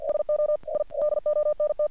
Morse code, heard as a keyed tone, about 52 wpm, 600 Hz. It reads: SO I DONT
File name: archived / CRSnet